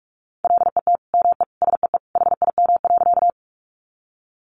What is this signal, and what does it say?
Morse code, heard as a keyed tone, about 44 wpm, 700 Hz. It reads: LET ME SEE HIM.
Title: Sentences from Top 300 Words 45wpm